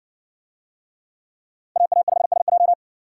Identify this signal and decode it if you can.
Morse code, heard as a keyed tone, about 60 wpm, 700 Hz. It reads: AA5SJ